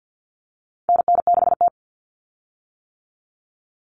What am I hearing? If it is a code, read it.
Morse code, heard as a keyed tone, about 50 wpm, 700 Hz. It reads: NN6T
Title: Call Signs - 50wpm